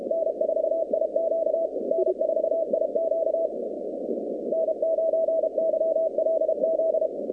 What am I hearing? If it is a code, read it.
Morse code, heard as a keyed tone, about 33 wpm, 650 Hz, with noise. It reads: N4IQ N4IQ N9YLZ
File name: 2019-cw-wpx-kt5j-k5pi